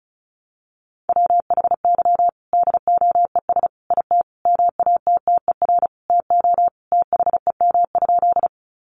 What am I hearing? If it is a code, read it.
Morse code, heard as a keyed tone, about 34 wpm, 700 Hz. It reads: WHY DOES IT MATTER TO THEM?
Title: Sentences from Top 700 Words 35wpm